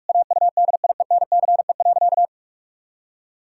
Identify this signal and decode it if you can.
Morse code, heard as a keyed tone, about 46 wpm, 700 Hz. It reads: AUDIENCE.
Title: Pangram Challenge v2 - 45wpm